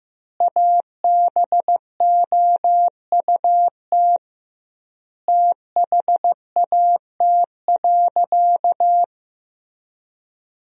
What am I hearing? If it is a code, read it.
Morse code, heard as a keyed tone, about 15 wpm, 700 Hz. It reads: ABOUT THAT.
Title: 2x Word-Spacing - Sentences from Top 400 Words - 15wpm